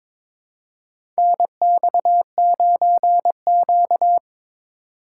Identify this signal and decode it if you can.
Morse code, heard as a keyed tone, about 22 wpm, 700 Hz. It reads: NX9Q
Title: USA Call Signs - 4 Characters - 3x Character Spacing - 22wpm